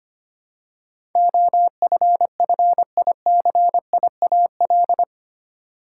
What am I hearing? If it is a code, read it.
Morse code, heard as a keyed tone, about 25 wpm, 700 Hz. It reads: OFFICIAL
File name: 8 Letter Words - First 3 Letters Given - 25wpm